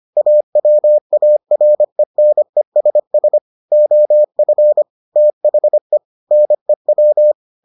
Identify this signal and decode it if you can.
Morse code, heard as a keyed tone, about 25 wpm, 600 Hz. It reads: AWARENESS OF THE NEW